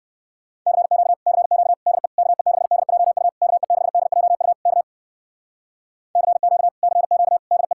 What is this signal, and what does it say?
Morse code, heard as a keyed tone, about 58 wpm, 700 Hz. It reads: CQ CQ DE KE6DJU KE6DJU K CQ CQ DE